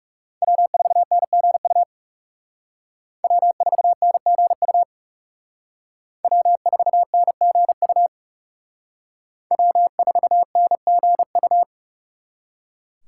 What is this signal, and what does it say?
Morse code, heard as a keyed tone, about 34 wpm, 700 Hz. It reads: W4NGU W4NGU W4NGU W4NGU
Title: Speed Racing - USA 1x3 Call Signs 30wpm